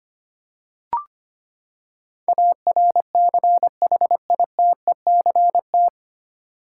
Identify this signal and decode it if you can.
Morse code, heard as a keyed tone, about 25 wpm, 700 Hz. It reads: ARCHITECT
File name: Nine Letter Words - 25wpm